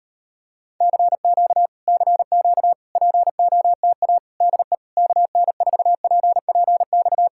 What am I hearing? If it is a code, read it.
Morse code, heard as a keyed tone, about 35 wpm, 700 Hz. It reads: CQ CQ POTA DE KN4PPX